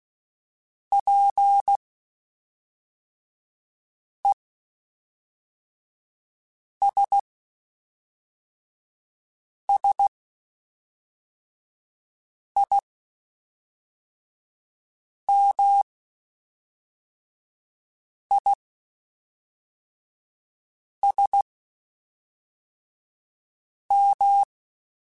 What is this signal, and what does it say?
Morse code, heard as a keyed tone, about 16 wpm, spaced as for 3 wpm, 800 Hz. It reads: PESSIMISM